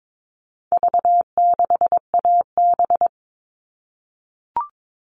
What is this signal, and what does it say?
Morse code, heard as a keyed tone, about 22 wpm, 700 Hz. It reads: V6AB
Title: All Cut Numbers - 4-Digit 22wpm